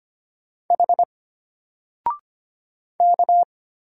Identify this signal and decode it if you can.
Morse code, heard as a keyed tone, about 25 wpm, 700 Hz. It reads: H K
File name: Single-Letter-Number-25wpm